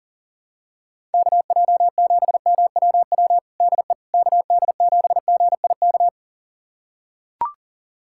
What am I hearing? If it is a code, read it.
Morse code, heard as a keyed tone, about 40 wpm, 700 Hz. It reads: KJ7MWW DE KD7GIK
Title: USA Callsign DE Callsign - 6 Characters - 40wpm